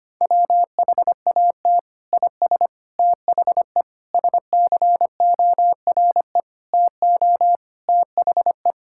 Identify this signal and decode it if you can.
Morse code, heard as a keyed tone, about 25 wpm, 700 Hz. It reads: WHAT IS THE SCORE TO THE
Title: Sentences from Top 1600 Words 25wpm